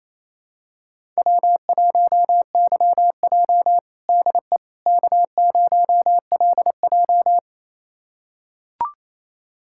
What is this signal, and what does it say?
Morse code, heard as a keyed tone, about 28 wpm, 700 Hz. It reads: W1YJ DE K0LJ
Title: USA Callsign DE Callsign - 4 Characters - 28wpm